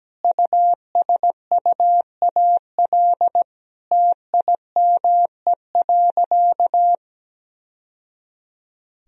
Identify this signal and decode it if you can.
Morse code, heard as a keyed tone, about 17 wpm, 700 Hz. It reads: USUAL TIME.
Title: Sentences from Top 1000 Words - Review Part 2 - 17wpm